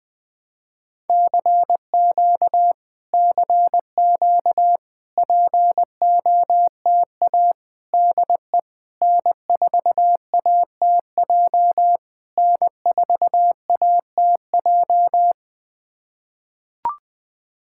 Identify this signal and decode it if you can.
Morse code, heard as a keyed tone, about 20 wpm, 700 Hz. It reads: CQ CQ POTA DE N4ATJ N4ATJ
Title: POTA QSO Practice - 20wpm